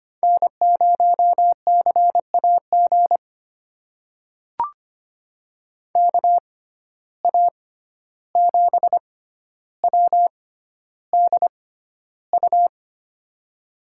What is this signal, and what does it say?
Morse code, heard as a keyed tone, about 25 wpm, spaced as for 12 wpm, 700 Hz. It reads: N0CAG KA7WDU
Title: USA Call Signs - 6x Character Spacing - 25wpm